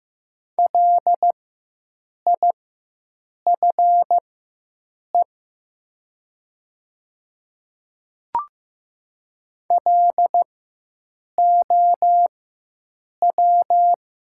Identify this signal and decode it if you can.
Morse code, heard as a keyed tone, about 15 wpm, spaced as for 7 wpm, 700 Hz. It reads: LIFE LOW
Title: 4x Character Spacing - Top 300 Words - 15wpm